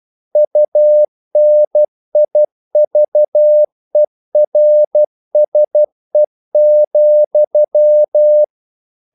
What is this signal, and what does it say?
Morse code, heard as a keyed tone, about 12 wpm, 600 Hz. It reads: UNIVERSE,
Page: English quotations in CW (telegraphy - morse)